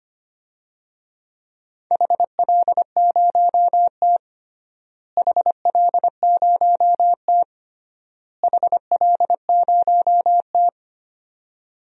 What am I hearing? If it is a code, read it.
Morse code, heard as a keyed tone, about 25 wpm, 700 Hz. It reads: HL0T HL0T HL0T
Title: DXCC-100 Call Signs - 4 Characters - Repeated 3x - 25wpm